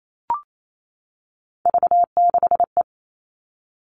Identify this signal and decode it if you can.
Morse code, heard as a keyed tone, about 28 wpm, 700 Hz. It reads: V6E
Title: All Cut Numbers - 3-Digit 28wpm